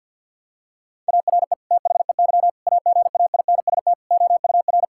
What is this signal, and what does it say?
Morse code, heard as a keyed tone, about 50 wpm, 700 Hz. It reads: ARE THEY AGAINST OUR
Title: Sentences from Top 1000 Words - Review Part 5 - 50wpm